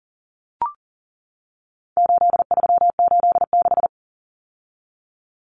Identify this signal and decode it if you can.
Morse code, heard as a keyed tone, about 40 wpm, 700 Hz. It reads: 8386
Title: Numbers - 4-Digit 40wpm